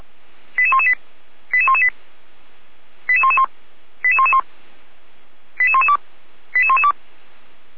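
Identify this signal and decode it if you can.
Five-tone selective call (ZVEI), heard as quick runs of five tones, tones of about 70 ms. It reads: 88118 88118 88111 88111 88112 88112
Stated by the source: ZVEI five-tone selective call code 88497, received by SDR, from fire brigade Feuerwehr Orlamünde's audio server.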